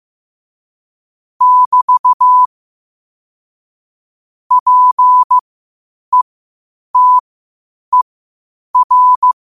Morse code, heard as a keyed tone, about 15 wpm, spaced as for 8 wpm, 1000 Hz. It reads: <BT> PETER
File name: qotdfiles / 8wpm